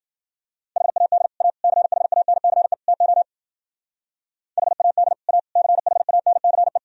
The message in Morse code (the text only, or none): HAD A CHANCE TO HAD A CHANCE